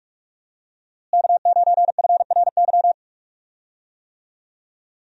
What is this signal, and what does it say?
Morse code, heard as a keyed tone, about 45 wpm, 700 Hz. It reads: K9FRY
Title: USA Call Signs - 5 Characters - Repeated 3x - 45wpm